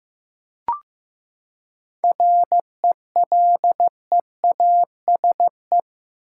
Morse code, heard as a keyed tone, about 16 wpm, 700 Hz. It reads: RELEASE
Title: Seven Letter Words - 15wpm